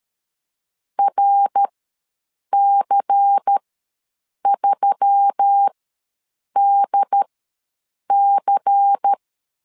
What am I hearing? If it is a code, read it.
Morse code, heard as a keyed tone, about 13 wpm, spaced as for 7 wpm, 800 Hz. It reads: RC3DC
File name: LEZIONI / 5 LBVC / 5 LBVC ws 12wpm 3159